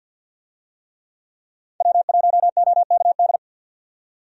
Morse code, heard as a keyed tone, about 48 wpm, 700 Hz. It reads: W1OKD
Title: USA Call Signs - 5 Characters - 50wpm